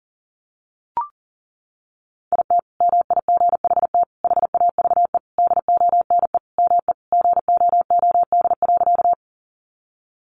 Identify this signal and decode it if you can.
Morse code, heard as a keyed tone, about 40 wpm, 700 Hz. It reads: IT MIGHT HAVE DONE ME GOOD.
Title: Sentences from Top 500 Words 40wpm